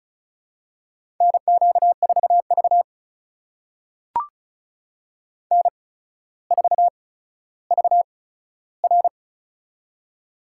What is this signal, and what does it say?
Morse code, heard as a keyed tone, about 36 wpm, spaced as for 15 wpm, 700 Hz. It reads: NQ4V N4VR
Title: USA Call Signs - 4 Characters - 8x Character Spacing - 35wpm